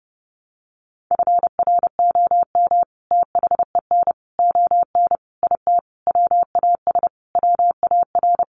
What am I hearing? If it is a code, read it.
Morse code, heard as a keyed tone, about 30 wpm, 700 Hz. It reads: FROM THEN ON IT WAS WAR
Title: Sentences from Top 400 Words 30wpm